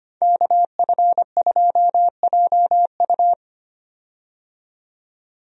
Morse code, heard as a keyed tone, about 25 wpm, 700 Hz. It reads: KF2JU
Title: USA Call Signs - 5 Characters - 25wpm